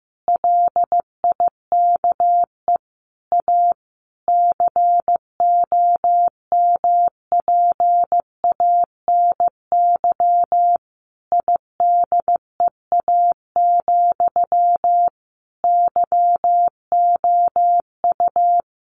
Morse code, heard as a keyed tone, about 15 wpm, 700 Hz. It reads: LIKE A COMPANY IDEA, YOU
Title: Sentences from Top 600 Words 15wpm